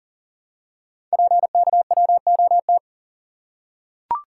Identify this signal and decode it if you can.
Morse code, heard as a keyed tone, about 39 wpm, 700 Hz. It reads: PKWOT
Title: Code Groups - 5 Letters - 40wpm